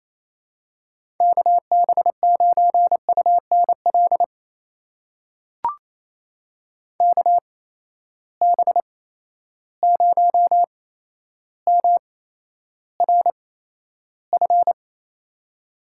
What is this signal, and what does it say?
Morse code, heard as a keyed tone, about 28 wpm, spaced as for 12 wpm, 700 Hz. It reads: KB9UNL KB0MRF
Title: USA Call Signs - 6 Characters - 8x Character Spacing - 28wpm